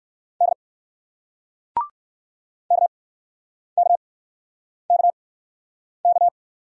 Morse code, heard as a keyed tone, about 59 wpm, spaced as for 9 wpm, 700 Hz. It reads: D KKKK